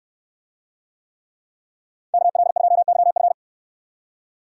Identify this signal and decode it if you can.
Morse code, heard as a keyed tone, about 68 wpm, 700 Hz. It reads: KL2YF